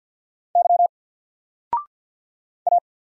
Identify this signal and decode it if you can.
Morse code, heard as a keyed tone, about 50 wpm, 700 Hz. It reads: Y A
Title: Single Letter-Number 50wpm